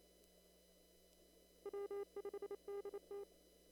Morse code, heard as a keyed tone, about 28 wpm, 400 Hz. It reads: W5DT
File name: CW / kt5j wpx cw-op-k5pi